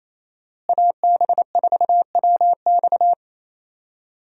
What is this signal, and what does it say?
Morse code, heard as a keyed tone, about 27 wpm, 700 Hz. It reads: AB4WX
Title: USA Call Signs - 5 Characters - Repeated 3x - 28wpm